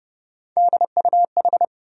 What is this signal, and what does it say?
Morse code, heard as a keyed tone, about 31 wpm, 700 Hz. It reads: DUH
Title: USA Call Sign Postfixes - 3 Letters 30wpm